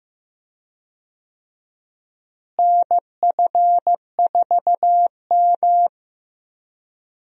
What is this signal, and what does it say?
Morse code, heard as a keyed tone, about 15 wpm, 700 Hz. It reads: NF4M
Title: USA Call Signs - 4 Characters - Repeated 2x - 15wpm